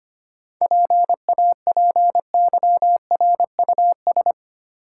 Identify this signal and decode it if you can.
Morse code, heard as a keyed tone, about 25 wpm, 700 Hz. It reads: PAPYRUS
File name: Easy Words - Speed-Racing with Hints - 25wpm